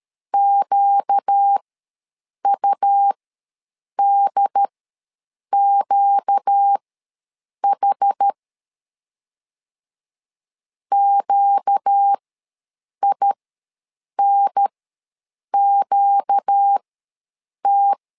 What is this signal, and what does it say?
Morse code, heard as a keyed tone, about 13 wpm, spaced as for 7 wpm, 800 Hz. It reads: QUDQH QINQT
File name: LEZIONI / 4 Q ws 12wpm 1912